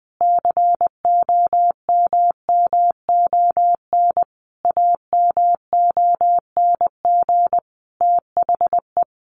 Morse code, heard as a keyed tone, about 20 wpm, 700 Hz. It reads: COMMON AMONG THE
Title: Sentences from Top 1400 Words 20wpm